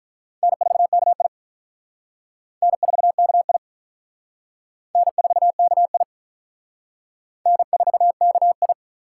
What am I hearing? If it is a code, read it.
Morse code, heard as a keyed tone, about 46 wpm, 700 Hz. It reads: N4KI N4KI N4KI N4KI